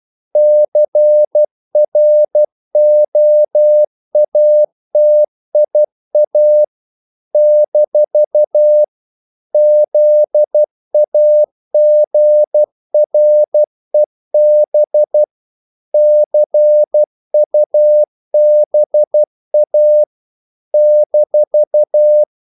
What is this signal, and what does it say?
Morse code, heard as a keyed tone, about 12 wpm, 600 Hz. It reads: CROATIA - ZAGREB CUBA -
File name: CW / Practice / NationalCapitals12